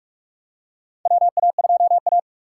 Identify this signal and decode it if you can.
Morse code, heard as a keyed tone, about 44 wpm, 700 Hz. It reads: WA2A
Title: USA Call Signs - 2x Character Spacing - 45wpm